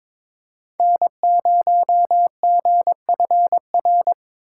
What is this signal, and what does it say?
Morse code, heard as a keyed tone, about 22 wpm, 700 Hz. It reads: N0GFR
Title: USA Call Signs - 5 Characters - 4x Character Spacing - 22wpm